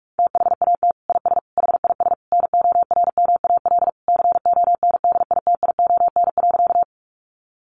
Morse code, heard as a keyed tone, about 44 wpm, 700 Hz. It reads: THAT IS HIS NORMAL CONDITION.
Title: Sentences from Top 1200 Words 45wpm